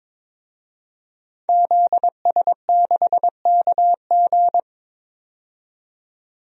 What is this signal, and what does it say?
Morse code, heard as a keyed tone, about 21 wpm, 700 Hz. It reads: ZS6KG
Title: DXCC-100 Call Signs - 5 Characters - 22wpm